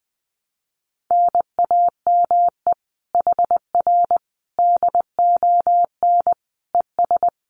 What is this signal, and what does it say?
Morse code, heard as a keyed tone, about 20 wpm, 700 Hz. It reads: NAME HR DON ES
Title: Common QSO Phrases Encore 20wpm